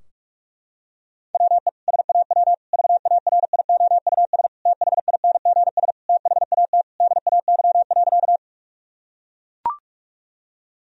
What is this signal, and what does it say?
Morse code, heard as a keyed tone, about 41 wpm, 700 Hz. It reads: WE SAW VARIOUS THINGS THAT DAY.
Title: Sentences from Top 1000 Words - Review Part 4 - 45wpm